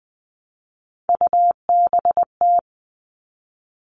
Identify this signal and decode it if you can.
Morse code, heard as a keyed tone, about 20 wpm, 700 Hz. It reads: UBT
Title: All Cut Numbers - 3-Digit 20wpm